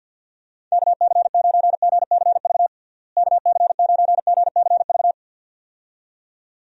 Morse code, heard as a keyed tone, about 52 wpm, 700 Hz. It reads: KC9GCV KC9GCV
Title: USA Call Signs - 6 Characters - Repeated 2x - 50wpm